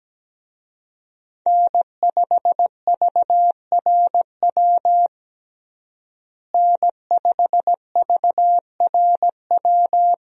Morse code, heard as a keyed tone, about 17 wpm, 700 Hz. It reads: N5VRW N5VRW
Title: USA Call Signs - 5 Characters - Repeated 3x - 17wpm